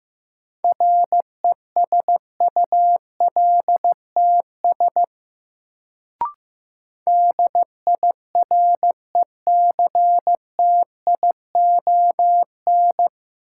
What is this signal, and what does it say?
Morse code, heard as a keyed tone, about 15 wpm, 700 Hz. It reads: RESULTS DIRECTION
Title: Sets of 4 Words - New Word Drill - 15wpm